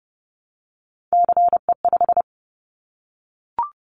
Morse code, heard as a keyed tone, about 31 wpm, 700 Hz. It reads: CE5
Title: Callsign Prefixes for Top 100 DXCC - 30wpm